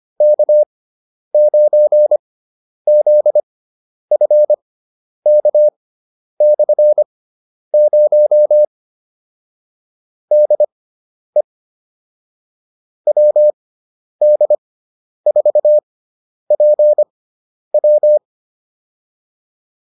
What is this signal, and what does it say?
Morse code, heard as a keyed tone, about 25 wpm, spaced as for 10 wpm, 600 Hz. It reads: K9ZFK/0 DE WD4PW
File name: QSO303 10